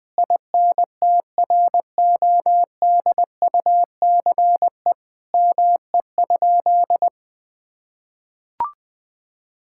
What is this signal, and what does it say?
Morse code, heard as a keyed tone, about 20 wpm, 700 Hz. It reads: INTRODUCE ME?
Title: Sentences from Top 1600 Words 20wpm